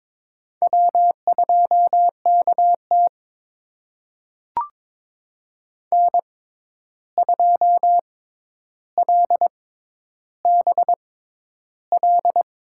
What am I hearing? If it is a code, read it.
Morse code, heard as a keyed tone, about 22 wpm, spaced as for 11 wpm, 700 Hz. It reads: W2KT N2LBL
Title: USA Call Signs - 6x Character Spacing - 22wpm